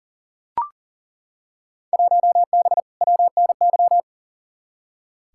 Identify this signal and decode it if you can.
Morse code, heard as a keyed tone, about 40 wpm, 700 Hz. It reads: 1B WNY
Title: Field Day Exchanges - 40wpm